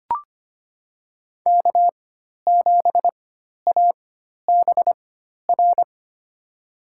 Morse code, heard as a keyed tone, about 25 wpm, spaced as for 12 wpm, 700 Hz. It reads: K7ABR
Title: USA Call Signs - 5 Characters - 4x Character Spacing - 25wpm